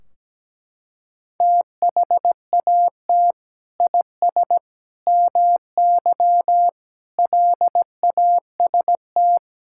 Morse code, heard as a keyed tone, about 17 wpm, 700 Hz. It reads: THAT IS MY LAST
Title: Sentences from Top 1600 Words 17wpm